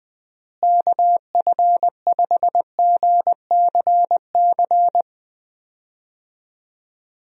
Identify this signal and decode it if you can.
Morse code, heard as a keyed tone, about 20 wpm, 700 Hz. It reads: KF5GCC
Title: USA Call Signs - 6 Characters - Repeated 2x - 20wpm